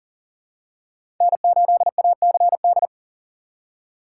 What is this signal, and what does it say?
Morse code, heard as a keyed tone, about 40 wpm, 700 Hz. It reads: N8ACD
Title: USA Call Signs - 5 Characters - 3x Character Spacing - 40wpm